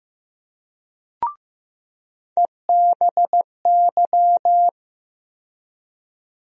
Morse code, heard as a keyed tone, about 15 wpm, 700 Hz. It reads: EBY
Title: USA Call Sign Postfixes - 3 Letters 15wpm